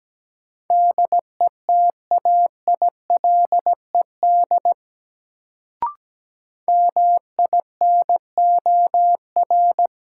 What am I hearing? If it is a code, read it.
Morse code, heard as a keyed tone, about 17 wpm, 700 Hz. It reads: DETAILED MINOR
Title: Sentences from Top 1500 Words - New Word Drill Alt - 17wpm